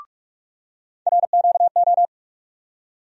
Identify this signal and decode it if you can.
Morse code, heard as a keyed tone, about 42 wpm, 700 Hz. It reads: RQO
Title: USA Call Sign Postfixes - 3 Letters 45wpm